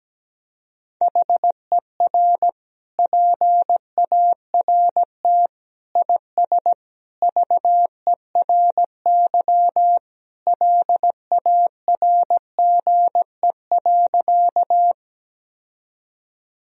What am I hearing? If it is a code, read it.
Morse code, heard as a keyed tone, about 17 wpm, 700 Hz. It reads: HER PART IS VERY LARGE.
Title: Sentences from Top 1000 Words - Review Part 1 - 17wpm